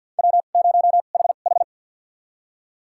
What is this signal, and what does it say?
Morse code, heard as a keyed tone, about 51 wpm, spaced as for 36 wpm, 700 Hz. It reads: W0HH
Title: USA Call Signs - 4 Characters - 2x Character Spacing - 50wpm